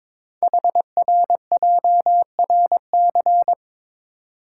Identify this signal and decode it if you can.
Morse code, heard as a keyed tone, about 22 wpm, 700 Hz. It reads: HRJRC
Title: Code Groups - 5 Letters - 22wpm